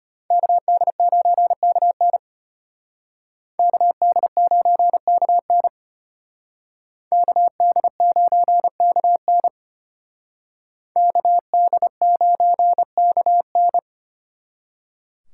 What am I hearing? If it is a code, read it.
Morse code, heard as a keyed tone, about 32 wpm, 700 Hz. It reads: KD9KN KD9KN KD9KN KD9KN